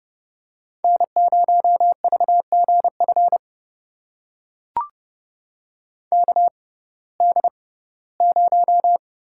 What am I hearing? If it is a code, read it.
Morse code, heard as a keyed tone, about 31 wpm, spaced as for 18 wpm, 700 Hz. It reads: N0VGF KD0
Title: USA Call Signs - 6x Character Spacing - 30wpm